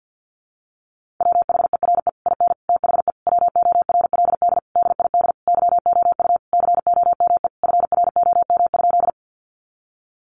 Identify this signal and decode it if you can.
Morse code, heard as a keyed tone, about 50 wpm, 700 Hz. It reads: WHERE IN THE WORLD DID YOU COME FROM?
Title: Sentences from Top 200 Words 50wpm